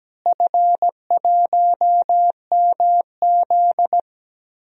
Morse code, heard as a keyed tone, about 17 wpm, 700 Hz. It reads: F1MZ
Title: DXCC-100 Call Signs - 4 Characters - Repeated 3x - 17wpm